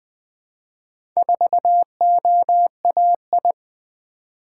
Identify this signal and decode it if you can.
Morse code, heard as a keyed tone, about 20 wpm, 700 Hz. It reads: 4OAI
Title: Four Letter-Number Combinations - 20wpm